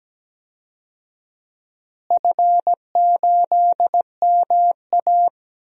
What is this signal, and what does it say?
Morse code, heard as a keyed tone, about 17 wpm, 700 Hz. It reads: F8MA
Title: DXCC-100 Call Signs - 4 Characters - Repeated 2x - 17wpm